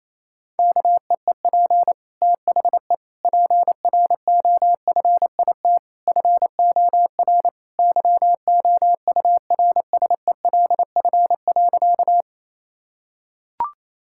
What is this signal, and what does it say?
Morse code, heard as a keyed tone, about 28 wpm, 700 Hz. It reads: KEEP THE PROFIT FOR YOURSELF.
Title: Sentences from Top 1000 Words - Review Part 5 - 28wpm